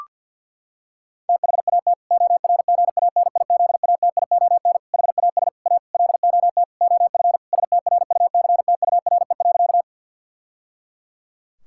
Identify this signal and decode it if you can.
Morse code, heard as a keyed tone, about 52 wpm, 700 Hz. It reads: THAT ORGANIZATION HAS A LOT OF STRUCTURE.
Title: Sentences from Top 1000 Words - Review Part 6 - 50wpm